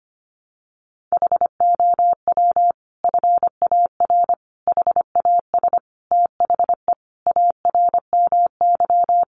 Morse code, heard as a keyed tone, about 25 wpm, 700 Hz. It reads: HOW FAR HAS THE ARMY
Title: Sentences from Top 1200 Words 25wpm